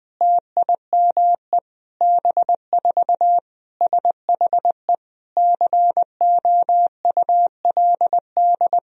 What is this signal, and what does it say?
Morse code, heard as a keyed tone, about 20 wpm, 700 Hz. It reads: TIME B4 SHE COULD
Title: Sentences from Top 500 Words and Abbreviations 20wpm